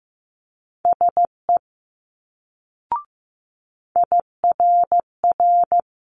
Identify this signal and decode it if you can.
Morse code, heard as a keyed tone, about 15 wpm, 700 Hz. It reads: SE IRR
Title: Multi-letter Phonemes Quick Fire 15wpm